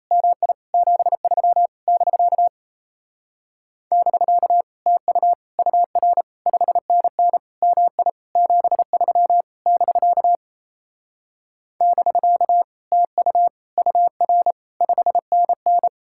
Morse code, heard as a keyed tone, about 37 wpm, 700 Hz. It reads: MI 73 <BK> <BK> TU UR 5NN MI 73 <BK> <BK> TU UR 5NN